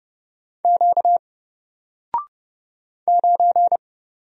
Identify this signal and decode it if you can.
Morse code, heard as a keyed tone, about 30 wpm, 700 Hz. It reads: Q 9
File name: Single-Letter-Number-30wpm